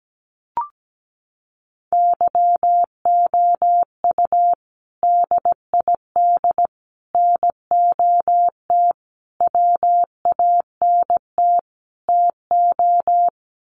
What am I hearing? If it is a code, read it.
Morse code, heard as a keyed tone, about 17 wpm, 700 Hz. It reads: YOU DID NOT WANT TO
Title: Sets of 5 Words Encore2 17wpm